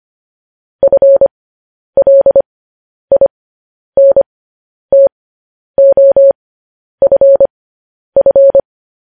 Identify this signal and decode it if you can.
Morse code, heard as a keyed tone, about 25 wpm, spaced as for 10 wpm, 550 Hz. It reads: FLINTOFF
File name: podcast / News headlines-1x-25x10